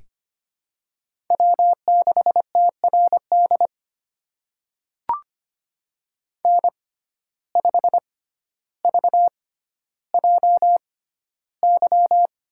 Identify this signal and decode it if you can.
Morse code, heard as a keyed tone, about 25 wpm, spaced as for 13 wpm, 700 Hz. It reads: W6TRD N5VJY
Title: USA Call Signs - 5 Characters - 6x Character Spacing - 25wpm